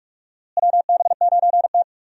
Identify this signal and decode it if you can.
Morse code, heard as a keyed tone, about 43 wpm, 700 Hz. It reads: WB9T
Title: USA Call Signs - 4 Characters - Repeated 2x - 45wpm